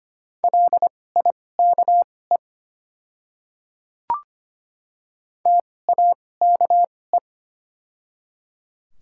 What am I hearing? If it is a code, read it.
Morse code, heard as a keyed tone, about 25 wpm, spaced as for 18 wpm, 700 Hz. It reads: LIKE TAKE
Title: 2x Character Spacing - Top 200 Words - 25wpm